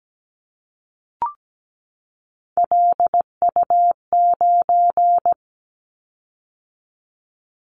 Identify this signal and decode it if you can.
Morse code, heard as a keyed tone, about 17 wpm, 700 Hz. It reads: LU9
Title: Callsign Prefixes for Top 50 DXCC - 17wpm